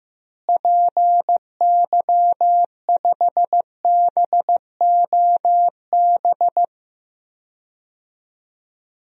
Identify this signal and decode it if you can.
Morse code, heard as a keyed tone, about 15 wpm, 700 Hz. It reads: PY5BOB
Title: DXCC-100 Call Signs - 6 Characters - Repeated 2x - 15wpm